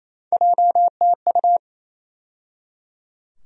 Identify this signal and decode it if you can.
Morse code, heard as a keyed tone, about 28 wpm, 700 Hz. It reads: JTU
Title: USA Call Sign Postfixes - 3 Letters 28wpm